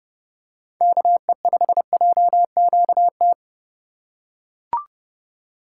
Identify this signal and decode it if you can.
Morse code, heard as a keyed tone, about 30 wpm, 700 Hz. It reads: KE5JQT
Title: USA Call Signs - 4x Character Spacing - 30wpm